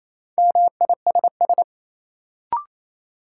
Morse code, heard as a keyed tone, about 28 wpm, 700 Hz. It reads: MISS
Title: Sentences from Top 500 Words - New Word Drill - 28wpm